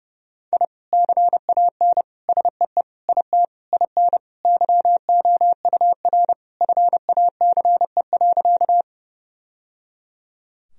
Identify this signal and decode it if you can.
Morse code, heard as a keyed tone, about 30 wpm, 700 Hz. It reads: I CAN SEE IT IN YOUR FACE.
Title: Sentences from Top 1000 Words - Review Part 2 - 30wpm